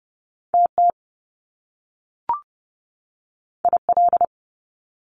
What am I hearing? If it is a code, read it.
Morse code, heard as a keyed tone, about 30 wpm, 700 Hz. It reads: TT IL
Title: Top Two Letter Combinations in Words 30wpm